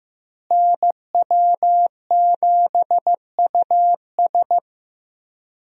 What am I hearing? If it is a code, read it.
Morse code, heard as a keyed tone, about 15 wpm, 700 Hz. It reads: NW7US
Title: USA Call Signs - 5 Characters - Repeated 3x - 15wpm